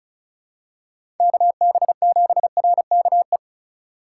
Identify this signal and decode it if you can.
Morse code, heard as a keyed tone, about 34 wpm, 700 Hz. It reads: KB7RKE
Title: USA Call Signs - 6 Characters - 4x Character Spacing - 35wpm